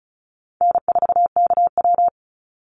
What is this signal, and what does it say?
Morse code, heard as a keyed tone, about 34 wpm, 700 Hz. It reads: N4KW
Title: Call Signs Repeated 3x - 35wpm